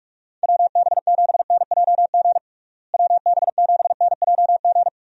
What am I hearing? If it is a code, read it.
Morse code, heard as a keyed tone, about 46 wpm, 700 Hz. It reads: WB7NJG WB7NJG
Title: USA Call Signs - 6 Characters - Repeated 3x - 45wpm